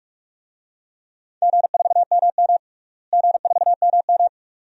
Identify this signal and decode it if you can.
Morse code, heard as a keyed tone, about 46 wpm, 700 Hz. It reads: G4MM G4MM
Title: DXCC-100 Call Signs - 4 Characters - Repeated 2x - 45wpm